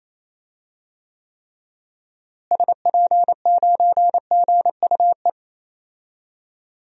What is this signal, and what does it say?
Morse code, heard as a keyed tone, about 28 wpm, 700 Hz. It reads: SP9GUE
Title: DXCC-100 Call Signs - 6 Characters - Repeated 3x - 28wpm